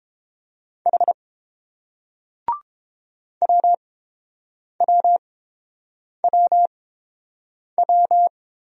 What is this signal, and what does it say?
Morse code, heard as a keyed tone, about 30 wpm, spaced as for 7 wpm, 700 Hz. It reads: H WWWW